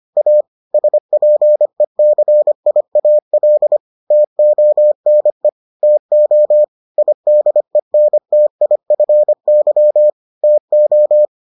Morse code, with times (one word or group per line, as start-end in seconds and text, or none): 0.2-0.4 A
0.7-3.8 SPECIAL
4.1-5.5 TONE
5.8-6.6 TO
7.0-10.1 IDENTIFY
10.4-11.3 TO